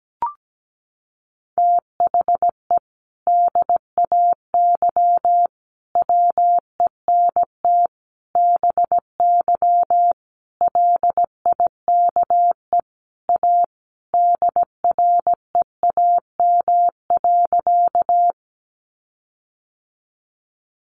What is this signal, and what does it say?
Morse code, heard as a keyed tone, about 17 wpm, 700 Hz. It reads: THE DAY WENT BY LIKE A DREAM.
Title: Sentences from Top 1500 Words 17wpm